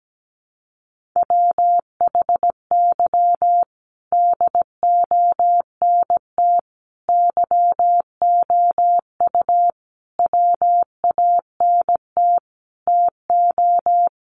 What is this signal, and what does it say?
Morse code, heard as a keyed tone, about 17 wpm, 700 Hz. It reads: WHY DONT YOU WANT TO
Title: Sentences from Top 300 Words 17wpm